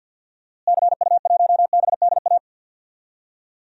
Morse code, heard as a keyed tone, about 46 wpm, 700 Hz. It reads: CU1BDA
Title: DXCC-100 Call Signs - 6 Characters - Repeated 2x - 50wpm